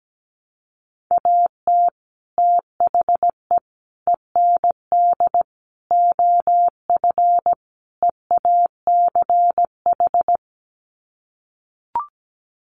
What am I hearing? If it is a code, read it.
Morse code, heard as a keyed tone, about 17 wpm, 700 Hz. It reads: AT THE END OF EACH
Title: Sets of 5 Words Encore 17wpm